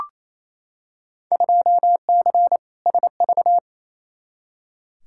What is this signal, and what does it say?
Morse code, heard as a keyed tone, about 28 wpm, 700 Hz. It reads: 2C SV
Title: Field Day Exchanges Encore - 28wpm